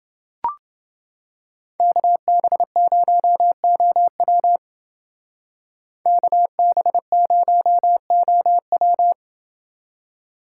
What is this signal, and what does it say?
Morse code, heard as a keyed tone, about 28 wpm, 700 Hz. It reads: KB0OW KB0OW